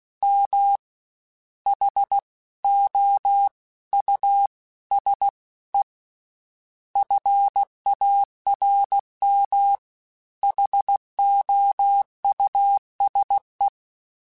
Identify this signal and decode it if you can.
Morse code, heard as a keyed tone, about 16 wpm, 800 Hz. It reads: M H O U S E FARM HOUSE
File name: General / General-mp3s / Houses @15